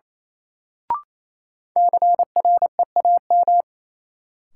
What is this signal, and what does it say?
Morse code, heard as a keyed tone, about 28 wpm, 700 Hz. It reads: CREAM
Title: Sentences from Top 1500 Words - New Word Drill - 28wpm